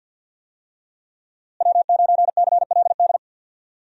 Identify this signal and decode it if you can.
Morse code, heard as a keyed tone, about 48 wpm, 700 Hz. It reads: W9CLD
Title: USA Call Signs - 5 Characters - Repeated 2x - 50wpm